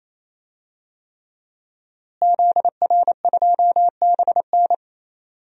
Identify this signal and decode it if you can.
Morse code, heard as a keyed tone, about 28 wpm, 700 Hz. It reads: ZR2BN
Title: DXCC-100 Call Signs - 5 Characters - Repeated 3x - 28wpm